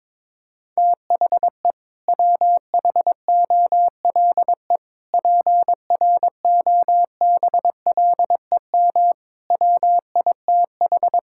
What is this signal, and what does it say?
Morse code, heard as a keyed tone, about 22 wpm, 700 Hz. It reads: THE WHOLE PROBLEM WITH
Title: Sentences from Top 1000 Words - Review Part 6 - 22wpm